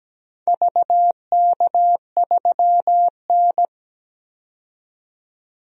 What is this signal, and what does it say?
Morse code, heard as a keyed tone, about 17 wpm, 700 Hz. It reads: VK3N
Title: DXCC-100 Call Signs - 4 Characters - 17wpm